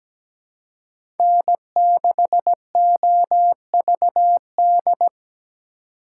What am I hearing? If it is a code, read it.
Morse code, heard as a keyed tone, about 20 wpm, 700 Hz. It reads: N6OVD
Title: Speed Racing - USA Call Signs 17wpm